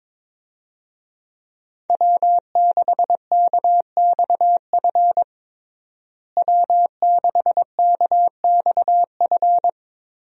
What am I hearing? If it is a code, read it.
Morse code, heard as a keyed tone, about 22 wpm, 700 Hz. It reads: W6KXF W6KXF
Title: USA Call Signs - 5 Characters - Repeated 2x - 22wpm